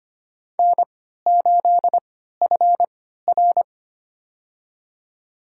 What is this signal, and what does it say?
Morse code, heard as a keyed tone, about 25 wpm, spaced as for 14 wpm, 700 Hz. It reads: N8FR
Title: USA Call Signs - 4 Characters - 3x Character Spacing - 25wpm